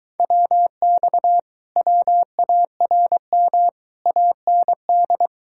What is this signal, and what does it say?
Morse code, heard as a keyed tone, about 23 wpm, 700 Hz. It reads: WX WARM AND